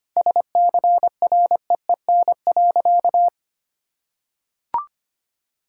Morse code, heard as a keyed tone, about 25 wpm, 700 Hz. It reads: SCREEN.
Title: Sentences from Top 1000 Words - Review Part 6 - 25wpm